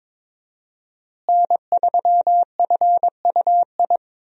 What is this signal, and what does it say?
Morse code, heard as a keyed tone, about 21 wpm, 700 Hz. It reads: N3FUI
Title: USA Call Signs - 5 Characters - Repeated 2x - 22wpm